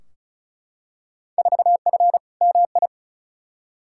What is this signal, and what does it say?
Morse code, heard as a keyed tone, about 34 wpm, 700 Hz. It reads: 4F MI
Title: Field Day Exchanges Encore - 35wpm